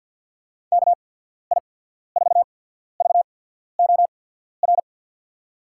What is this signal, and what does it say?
Morse code, heard as a keyed tone, about 49 wpm, spaced as for 14 wpm, 700 Hz. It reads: KI4VOR
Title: USA Call Signs - 6 Characters - 8x Character Spacing - 50wpm